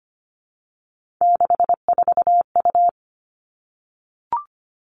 Morse code, heard as a keyed tone, about 25 wpm, 700 Hz. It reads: 64U
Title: Three Challenging Letter-Number Combinations 25wpm